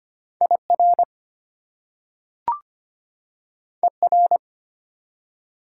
Morse code, heard as a keyed tone, about 25 wpm, 700 Hz. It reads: IR ER
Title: Prefixes and Suffixes 25wpm